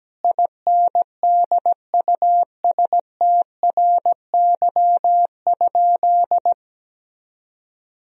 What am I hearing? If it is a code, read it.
Morse code, heard as a keyed tone, about 17 wpm, 700 Hz. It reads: INDUSTRY?
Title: Sentences from Top 1000 Words - Review Part 4 - 17wpm